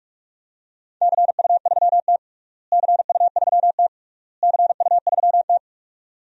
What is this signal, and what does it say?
Morse code, heard as a keyed tone, about 41 wpm, 700 Hz. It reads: CU3T CU3T CU3T
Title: DXCC-100 Call Signs - 4 Characters - Repeated 3x - 45wpm